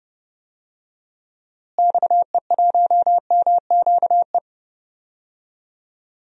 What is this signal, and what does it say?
Morse code, heard as a keyed tone, about 30 wpm, 700 Hz. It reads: XE1MQE
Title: DXCC-100 Call Signs - 6 Characters - Repeated 3x - 30wpm